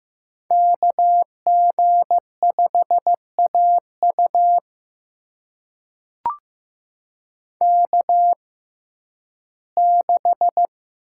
Morse code, heard as a keyed tone, about 15 wpm, spaced as for 11 wpm, 700 Hz. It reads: KG5AU K6
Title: USA Call Signs - 6x Character Spacing - 15wpm